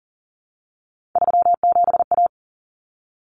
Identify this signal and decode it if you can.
Morse code, heard as a keyed tone, about 40 wpm, 700 Hz. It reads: 37A
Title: Common Cut Numbers - 3-Digit 40wpm